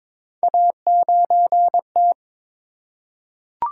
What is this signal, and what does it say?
Morse code, heard as a keyed tone, about 23 wpm, 700 Hz. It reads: A9T
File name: practice / 22wpm / Three Letter-Number Combinations - 22wpm